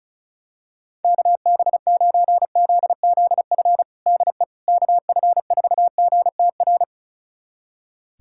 Morse code, heard as a keyed tone, about 36 wpm, 700 Hz. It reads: KB9ZZF DE KF4GTR
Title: USA Callsign DE Callsign - 6 Characters - 35wpm